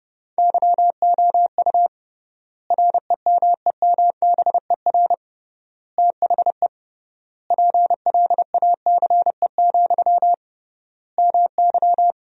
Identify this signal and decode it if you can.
Morse code, heard as a keyed tone, about 32 wpm, 700 Hz. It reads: YOU REMEMBER THE PLACE, MY
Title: 3x Word-Spacing - Sentences from Top 400 Words - 30wpm